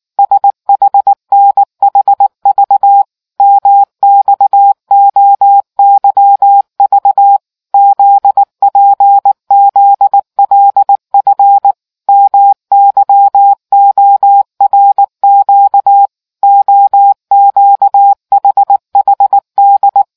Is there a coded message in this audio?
Yes: Morse code from 0.2 to 20.0 s, heard as a keyed tone, about 19 wpm, 800 Hz. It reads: SHNHV MXOYV ZPZLF MYORQ OQHHD